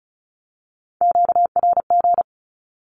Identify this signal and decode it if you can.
Morse code, heard as a keyed tone, about 34 wpm, 700 Hz. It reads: QRG
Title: Q-Codes with Spoken Meaning - 35wpm